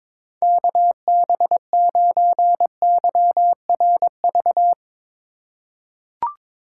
Morse code, heard as a keyed tone, about 22 wpm, 700 Hz. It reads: KB9YRV
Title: USA Call Signs - 6 Characters - 4x Character Spacing - 22wpm